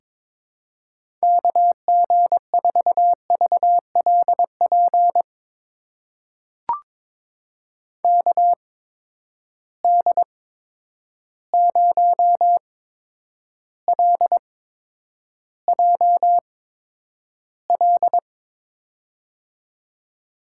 Morse code, heard as a keyed tone, about 22 wpm, spaced as for 9 wpm, 700 Hz. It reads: KG4VLP KD0LJL
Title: USA Call Signs - 6 Characters - 8x Character Spacing - 22wpm